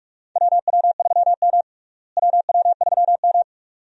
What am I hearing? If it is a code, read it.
Morse code, heard as a keyed tone, about 41 wpm, 700 Hz. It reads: WW3M WW3M
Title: USA Call Signs - 4 Characters - Repeated 2x - 45wpm